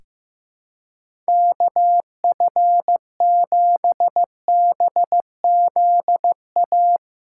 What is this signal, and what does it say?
Morse code, heard as a keyed tone, about 15 wpm, 700 Hz. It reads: KF7BZA
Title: USA Callsign DE Callsign - 6 Characters - 15wpm